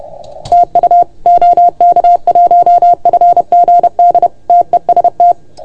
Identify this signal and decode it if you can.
Morse code, heard as a keyed tone, about 30 wpm, 690 Hz. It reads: TU OK1FGD TEST
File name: Archiv / CUC381 / OK1FGD CUC 381